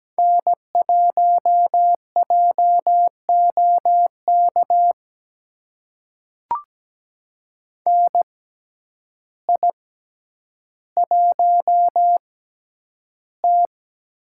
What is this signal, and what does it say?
Morse code, heard as a keyed tone, about 17 wpm, spaced as for 9 wpm, 700 Hz. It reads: N1JOK NI1T
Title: USA Call Signs - 6x Character Spacing - 17wpm